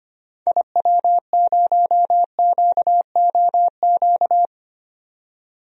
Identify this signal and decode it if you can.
Morse code, heard as a keyed tone, about 25 wpm, 700 Hz. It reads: IW0QOQ
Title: DXCC-100 Call Signs - 6 Characters - 25wpm